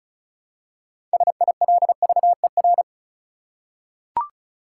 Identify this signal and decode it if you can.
Morse code, heard as a keyed tone, about 34 wpm, 700 Hz. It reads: SILVER
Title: Six Letter Words - 35wpm